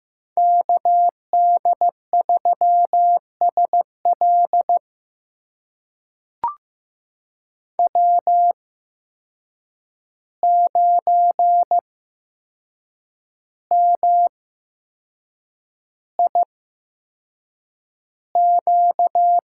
Morse code, heard as a keyed tone, about 15 wpm, spaced as for 6 wpm, 700 Hz. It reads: KD3SL W9MIQ
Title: USA Call Signs - 5 Characters - 8x Character Spacing - 15wpm